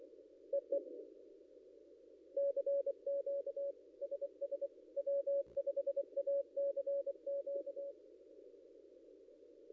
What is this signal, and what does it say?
Morse code, heard as a keyed tone, about 25 wpm, 560 Hz. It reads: EE CQ SS W5ACQ
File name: contests / ss-cw-k5tr-op-n5rz -0800